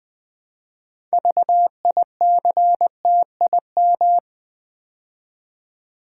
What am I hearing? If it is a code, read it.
Morse code, heard as a keyed tone, about 20 wpm, 700 Hz. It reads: VICTIM
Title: Build A Word - 3 Letters At A Time - 6-Letter Words - 20wpm